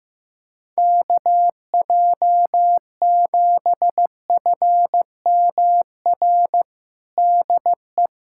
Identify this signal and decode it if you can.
Morse code, heard as a keyed tone, about 15 wpm, 700 Hz. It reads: KJ7FMR DE
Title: USA Callsign DE Callsign 15wpm